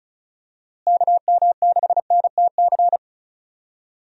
Morse code, heard as a keyed tone, about 36 wpm, 700 Hz. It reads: KM6NTC
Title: USA Call Signs - 6 Characters - Repeated 3x - 35wpm